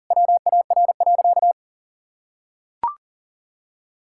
Morse code, heard as a keyed tone, about 37 wpm, 700 Hz. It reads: WAR.
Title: Sentences from Top 1000 Words - Review Part 6 - 40wpm